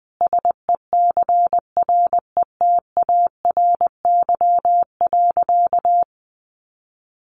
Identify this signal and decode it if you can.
Morse code, heard as a keyed tone, about 20 wpm, 700 Hz. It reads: SECRETARY.
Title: Sentences from Top 1300 Words 20wpm